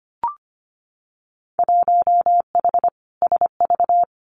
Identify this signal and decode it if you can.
Morse code, heard as a keyed tone, about 25 wpm, 700 Hz. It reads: 1H SV
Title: Winter Field Day Exchanges 25wpm